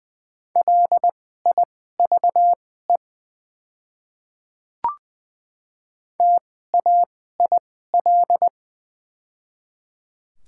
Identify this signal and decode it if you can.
Morse code, heard as a keyed tone, about 20 wpm, spaced as for 14 wpm, 690 Hz. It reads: LIVE TAIL
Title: 2x Character Spacing - Top 500 Words - 20wpm